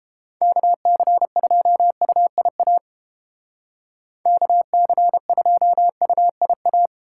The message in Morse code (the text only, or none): KC2UIA KC2UIA